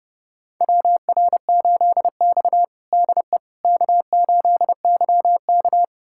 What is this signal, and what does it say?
Morse code, heard as a keyed tone, about 30 wpm, 700 Hz. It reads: WR8X DE K8YK
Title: USA Callsign DE Callsign - 4 Characters - 30wpm